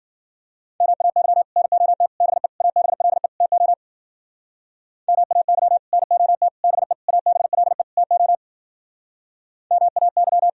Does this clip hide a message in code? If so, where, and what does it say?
Yes: Morse code from 0.8 to 10.5 s, heard as a keyed tone, about 53 wpm, 700 Hz. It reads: MAY NOT BE ABLE TO MAY NOT BE ABLE TO MAY